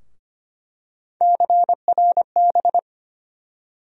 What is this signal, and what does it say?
Morse code, heard as a keyed tone, about 25 wpm, 700 Hz. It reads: CRB
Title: USA Call Sign Postfixes - 3 Letters 25wpm